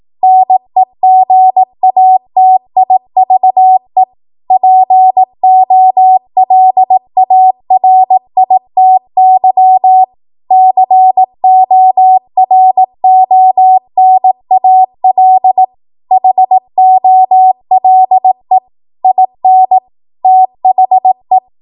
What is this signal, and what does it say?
Morse code, heard as a keyed tone, about 18 wpm, 750 Hz. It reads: NEGATIVE POLARITY CORONAL HOLE IN THE
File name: Morse / 18WPM